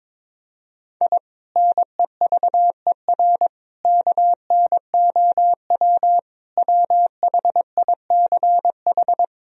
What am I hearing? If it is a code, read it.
Morse code, heard as a keyed tone, about 22 wpm, 700 Hz. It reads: I NEVER KNOW WHICH
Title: Sentences from Top 1000 Words - Review Part 3 - 22wpm